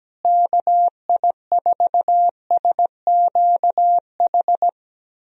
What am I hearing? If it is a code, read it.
Morse code, heard as a keyed tone, about 17 wpm, 700 Hz. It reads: KI4SQH
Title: USA Call Signs - 6 Characters - Repeated 3x - 17wpm